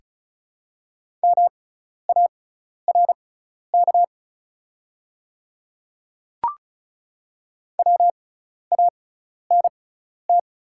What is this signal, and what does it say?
Morse code, heard as a keyed tone, about 36 wpm, spaced as for 12 wpm, 700 Hz. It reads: MARK WANT
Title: 6x Character Spacing - Top 300 Words - 35wpm